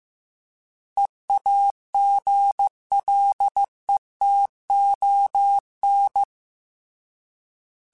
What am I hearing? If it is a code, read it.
Morse code, heard as a keyed tone, about 15 wpm, 800 Hz. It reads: EAGLETON
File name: fromSMs / FromSMs mp3s / Cities@13